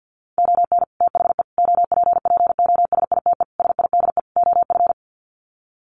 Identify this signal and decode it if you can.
Morse code, heard as a keyed tone, about 48 wpm, 700 Hz. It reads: ON THE OPPOSITE SIDE OF
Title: Sets of 5 Words Encore2 50wpm